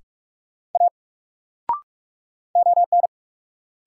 Morse code, heard as a keyed tone, about 42 wpm, 700 Hz. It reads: A ON
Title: Sentences from Top 100 Words - New Word Drill - 45wpm